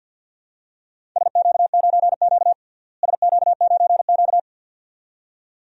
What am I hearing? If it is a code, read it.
Morse code, heard as a keyed tone, about 47 wpm, 700 Hz. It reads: SQ9Q SQ9Q
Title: DXCC-100 Call Signs - 4 Characters - Repeated 2x - 50wpm